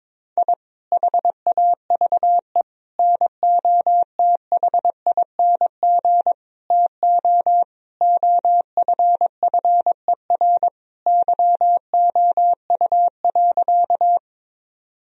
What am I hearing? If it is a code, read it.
Morse code, heard as a keyed tone, about 21 wpm, 700 Hz. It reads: I HAVE NOTHING TO OFFER YOU.
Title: Sentences from Top 1000 Words - Review Part 4 - 22wpm